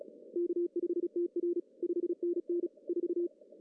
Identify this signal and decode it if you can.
Morse code, heard as a keyed tone, about 36 wpm, 350 Hz. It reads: K5TR 5NN 4
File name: CW / cqww-cw-k5tr